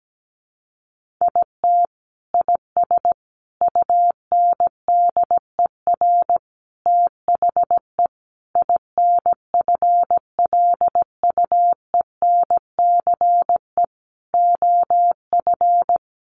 Morse code, heard as a keyed tone, about 17 wpm, 700 Hz. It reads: IT IS UNDER THE INFLUENCE OF
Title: Sentences from Top 1400 Words 17wpm